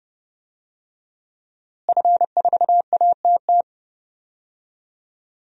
Morse code, heard as a keyed tone, about 30 wpm, 700 Hz. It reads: F4ATT
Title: DXCC-100 Call Signs - 5 Characters - Repeated 3x - 30wpm